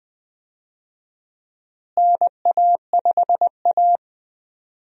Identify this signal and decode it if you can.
Morse code, heard as a keyed tone, about 20 wpm, 700 Hz. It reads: NA5A
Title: USA Call Signs - 4 Characters - Repeated 2x - 20wpm